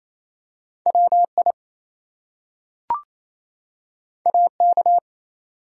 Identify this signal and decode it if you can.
Morse code, heard as a keyed tone, about 28 wpm, 700 Hz. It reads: WI AK
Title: US States and Canadian Provinces 28wpm